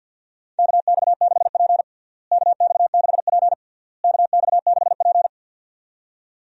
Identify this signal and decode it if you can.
Morse code, heard as a keyed tone, about 51 wpm, 700 Hz. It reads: KX6P KX6P KX6P
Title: USA Call Signs - 4 Characters - Repeated 3x - 50wpm